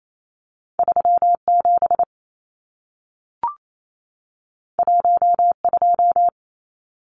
Morse code, heard as a keyed tone, about 28 wpm, 700 Hz. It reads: 37 12
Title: Numbers - 2-Digit 28wpm